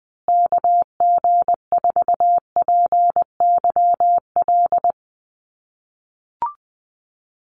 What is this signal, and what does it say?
Morse code, heard as a keyed tone, about 20 wpm, 700 Hz. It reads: KG4PYL
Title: USA Call Signs Repeated 2x 20wpm